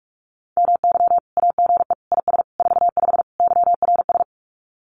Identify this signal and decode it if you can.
Morse code, heard as a keyed tone, about 44 wpm, 700 Hz. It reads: MY AGE IS 45 YRS
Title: Common QSO Phrases Encore 45wpm